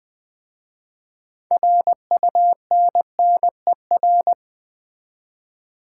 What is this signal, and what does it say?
Morse code, heard as a keyed tone, about 20 wpm, 700 Hz. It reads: RUNNER
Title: 6 Letter Words - First 3 Letters Given - 20wpm